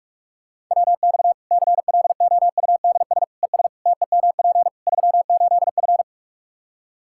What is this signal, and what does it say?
Morse code, heard as a keyed tone, about 46 wpm, 700 Hz. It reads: WX CLOUDS ES TEMP 38F
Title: Practice QSO - Exchange 2 - Info - 45wpm